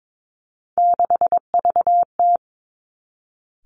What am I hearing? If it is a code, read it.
Morse code, heard as a keyed tone, about 22 wpm, 700 Hz. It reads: 6VT
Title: Three Challenging Letter-Number Combinations 22wpm